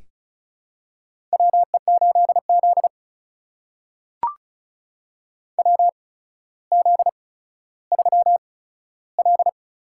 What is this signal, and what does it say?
Morse code, heard as a keyed tone, about 36 wpm, spaced as for 15 wpm, 700 Hz. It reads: WE8Z WZ3L
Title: USA Call Signs - 4 Characters - 8x Character Spacing - 35wpm